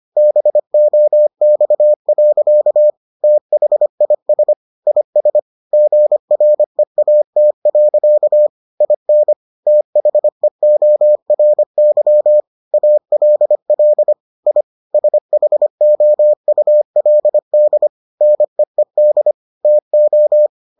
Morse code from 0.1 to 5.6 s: BOX. THIS IS

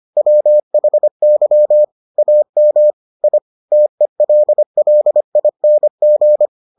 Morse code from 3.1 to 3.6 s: I

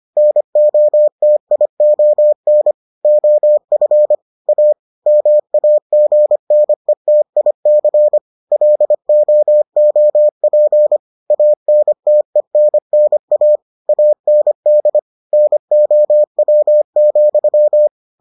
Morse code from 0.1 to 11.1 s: NOTION OF A MAGNETIC LOOP

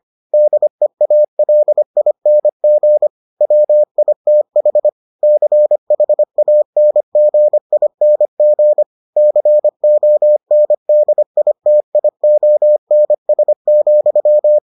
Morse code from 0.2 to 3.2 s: DEALING

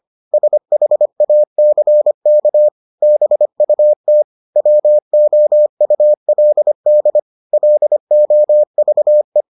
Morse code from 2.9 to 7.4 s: BUT WOULD

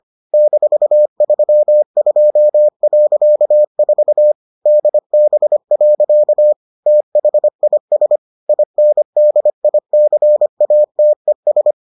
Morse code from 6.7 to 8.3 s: THIS

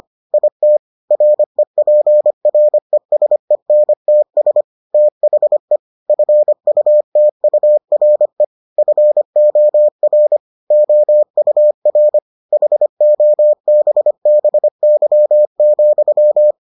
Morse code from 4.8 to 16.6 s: THE FUTURE FOR OUR HOBBY,

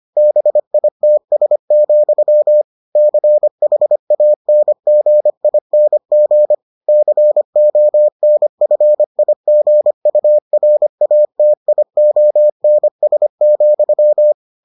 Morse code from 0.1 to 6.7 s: BITS, CHANGING

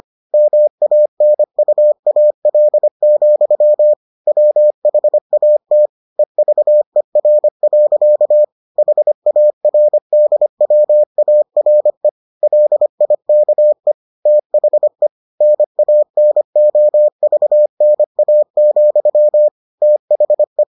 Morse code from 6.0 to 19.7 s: EVER. HARDWARE LIKE THE NANOVNA,

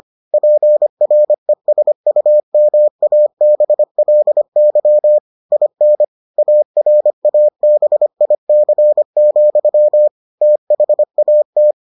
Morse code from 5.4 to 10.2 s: IN ARABIC,

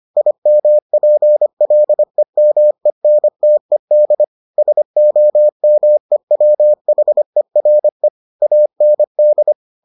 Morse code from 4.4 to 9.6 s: SOMEWHERE AND

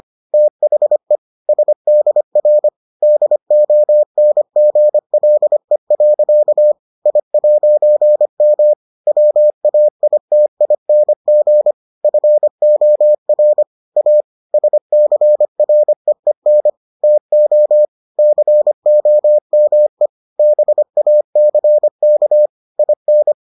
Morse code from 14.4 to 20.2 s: SCREEN TO COME

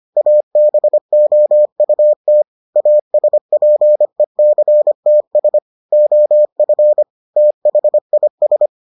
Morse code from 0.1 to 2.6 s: ABOUT

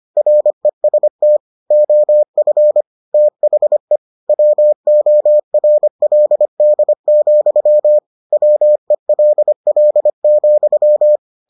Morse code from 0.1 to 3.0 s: REST OF

